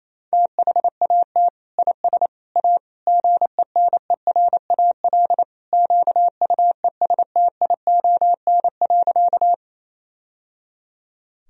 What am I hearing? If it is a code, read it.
Morse code, heard as a keyed tone, about 28 wpm, 700 Hz. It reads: THAT IS A GENERAL QUESTION.
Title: Sentences from Top 1000 Words - Review Part 3 - 28wpm